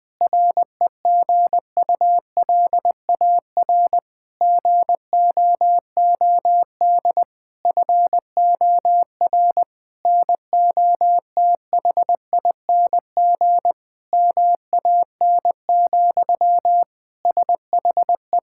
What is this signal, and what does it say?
Morse code, heard as a keyed tone, about 21 wpm, 700 Hz. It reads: REGULAR GOOD FOR NOTHING MAN, SHE
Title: Sentences from Top 1000 Words - Review Part 6 - 20wpm